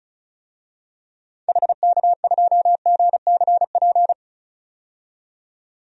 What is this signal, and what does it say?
Morse code, heard as a keyed tone, about 36 wpm, 700 Hz. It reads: HK2GCP
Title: DXCC-100 Call Signs - 6 Characters - 35wpm